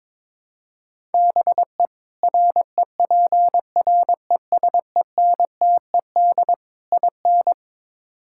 Morse code, heard as a keyed tone, about 22 wpm, 700 Hz. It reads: BE REPRESENTED IN
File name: practice / 22wpm / Build A Phrase - 5-Word Phrases - 22wpm